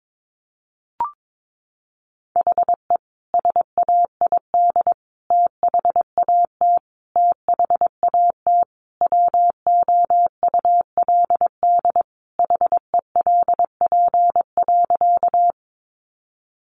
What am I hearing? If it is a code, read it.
Morse code, heard as a keyed tone, about 22 wpm, 700 Hz. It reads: HE SAID THAT THAT WOULD HELP.
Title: Sentences from Top 200 Words 22wpm